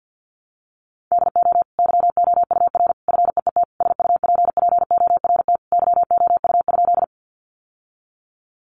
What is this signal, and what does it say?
Morse code, heard as a keyed tone, about 50 wpm, 700 Hz. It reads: DO YOUR FEET SUPPORT YOU?
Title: Sentences from Top 800 Words 50wpm